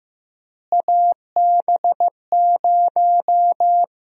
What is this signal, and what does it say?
Morse code, heard as a keyed tone, about 15 wpm, 700 Hz. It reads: AB0
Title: USA Call Sign Prefixes - Repeated 2x - 15wpm